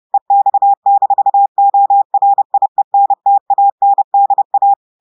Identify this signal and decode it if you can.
Morse code, heard as a keyed tone, about 30 wpm, 800 Hz. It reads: EX-ORIENTANDA
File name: audios / portugais / portugais texte en clair 1 vitesse 30 mots minutes